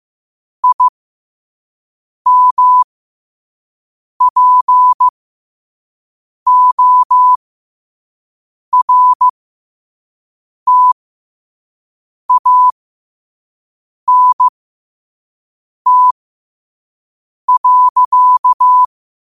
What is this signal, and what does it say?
Morse code, heard as a keyed tone, about 15 wpm, spaced as for 5 wpm, 1000 Hz. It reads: IMPORTANT.
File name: qotdfiles / 5wpm